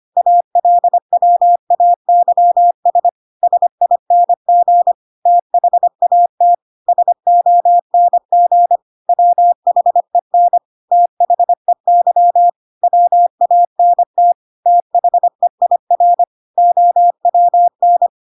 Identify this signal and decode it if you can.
Morse code, heard as a keyed tone, about 25 wpm, 700 Hz. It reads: ALWAYS SING THAT SONG WHEN THEY WANT THEIR OWN
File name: Pinocchio / Pinocchio ch8 25 25 ik1hge